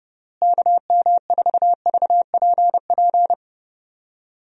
Morse code, heard as a keyed tone, about 30 wpm, 700 Hz. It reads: KM4VPP
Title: USA Call Signs - 6 Characters - Repeated 3x - 30wpm